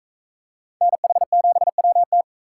Morse code, heard as a keyed tone, about 43 wpm, 700 Hz. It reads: NH7WT